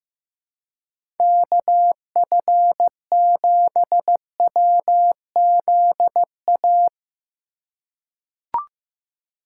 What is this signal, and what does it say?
Morse code, heard as a keyed tone, about 15 wpm, 700 Hz. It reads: KF7WZA
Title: USA Call Signs - 6 Characters - 4x Character Spacing - 15wpm